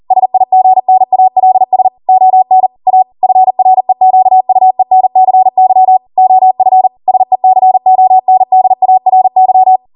Morse code, heard as a keyed tone, about 40 wpm, 750 Hz. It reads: SIGNALS ON A FREQUENCY OF SECONDARY
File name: Morse / 40WPM